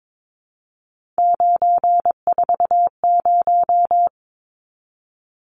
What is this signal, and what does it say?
Morse code, heard as a keyed tone, about 22 wpm, 700 Hz. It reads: 940
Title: Numbers - 3-Digit 22wpm